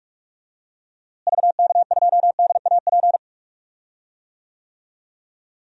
Morse code, heard as a keyed tone, about 45 wpm, 700 Hz. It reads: VK2DAP
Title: DXCC-100 Call Signs - 6 Characters - Repeated 3x - 45wpm